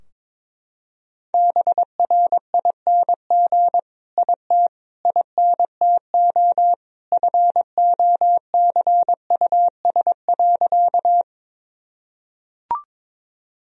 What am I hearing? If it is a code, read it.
Morse code, heard as a keyed tone, about 22 wpm, 700 Hz. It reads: BRING IT INTO FOCUS.
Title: Sentences from Top 1000 Words - Review Part 4 - 22wpm